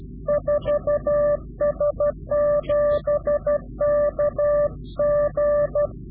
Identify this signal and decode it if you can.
Morse code, heard as a keyed tone, about 12 wpm, 590 Hz. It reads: 4S7KG
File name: CW-QSO / Directional Call / Directional QSO